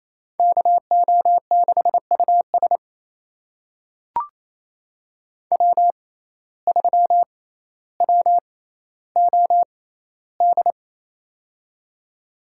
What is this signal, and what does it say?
Morse code, heard as a keyed tone, about 28 wpm, spaced as for 14 wpm, 700 Hz. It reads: KO6US W3WOD
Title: USA Call Signs - 5 Characters - 6x Character Spacing - 28wpm